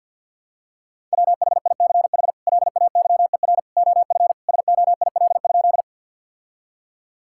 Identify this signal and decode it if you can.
Morse code, heard as a keyed tone, about 50 wpm, 700 Hz. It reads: WHICH LAYER OF SOIL?
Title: Sentences from Top 1600 Words 50wpm